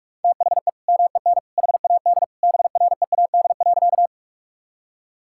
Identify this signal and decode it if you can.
Morse code, heard as a keyed tone, about 41 wpm, 700 Hz. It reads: THE MEN HAD BREAD.
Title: Sentences from Top 1600 Words 45wpm